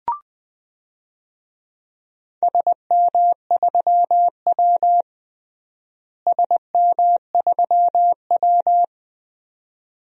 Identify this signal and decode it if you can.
Morse code, heard as a keyed tone, about 20 wpm, 700 Hz. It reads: SM3W SM3W
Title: DXCC-100 Call Signs - 4 Characters - Repeated 2x - 20wpm